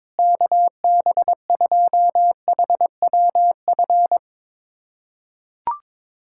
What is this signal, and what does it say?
Morse code, heard as a keyed tone, about 22 wpm, 700 Hz. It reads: KB2HWF
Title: USA Callsign DE Callsign - 6 Characters - 22wpm